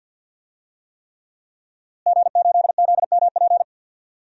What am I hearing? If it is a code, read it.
Morse code, heard as a keyed tone, about 52 wpm, 700 Hz. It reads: G8ZMP
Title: DXCC-100 Call Signs - 5 Characters - Repeated 3x - 50wpm